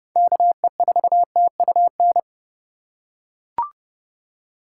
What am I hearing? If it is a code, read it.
Morse code, heard as a keyed tone, about 30 wpm, 700 Hz. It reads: KE4TUN
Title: USA Call Signs - 6 Characters - 3x Character Spacing - 30wpm